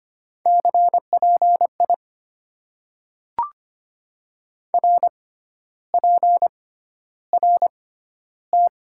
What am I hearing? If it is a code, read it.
Morse code, heard as a keyed tone, about 25 wpm, spaced as for 12 wpm, 700 Hz. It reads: CPI RPRT
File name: Top 90 QSO Elements - 6x Character Spacing - 25wpm